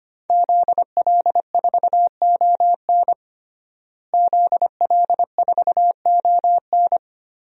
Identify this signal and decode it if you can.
Morse code, heard as a keyed tone, about 25 wpm, 700 Hz. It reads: ZL4ON ZL4ON
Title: DXCC-100 Call Signs - 5 Characters - Repeated 3x - 25wpm